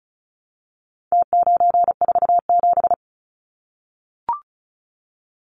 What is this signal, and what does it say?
Morse code, heard as a keyed tone, about 34 wpm, 700 Hz. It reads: T947
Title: Numbers - Contest Style - 4-Digit 35wpm